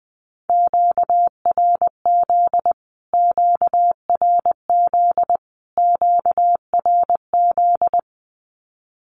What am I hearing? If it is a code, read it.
Morse code, heard as a keyed tone, about 20 wpm, 700 Hz. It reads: QRZ QRZ QRZ
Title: Q-Codes Repeated 3x 20wpm